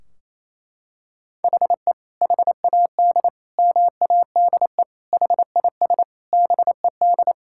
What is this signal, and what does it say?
Morse code, heard as a keyed tone, about 27 wpm, 700 Hz. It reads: HE HAD MADE HIS BED
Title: Sentences from Top 1600 Words 28wpm